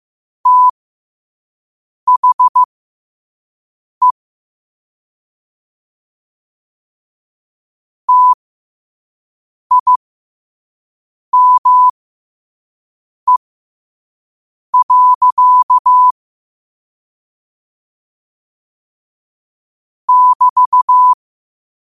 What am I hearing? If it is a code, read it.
Morse code, heard as a keyed tone, about 15 wpm, spaced as for 5 wpm, 1000 Hz. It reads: THE TIME. <BT>
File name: qotdfiles / 5wpm